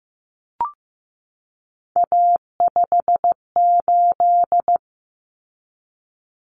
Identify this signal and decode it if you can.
Morse code, heard as a keyed tone, about 15 wpm, 700 Hz. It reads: A58
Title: Common Cut Numbers - 3-Digit 15wpm